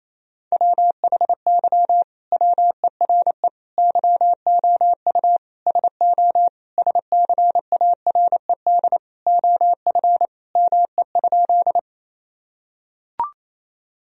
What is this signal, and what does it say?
Morse code, heard as a keyed tone, about 28 wpm, 700 Hz. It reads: WHY WERE YOU SO SCARED OF ME?
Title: Sentences from Top 1000 Words - Review Part 4 - 28wpm